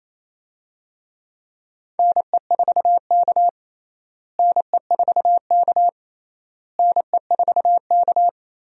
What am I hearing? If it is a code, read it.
Morse code, heard as a keyed tone, about 28 wpm, 700 Hz. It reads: NE4K NE4K NE4K
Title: USA Call Signs - 4 Characters - Repeated 3x - 28wpm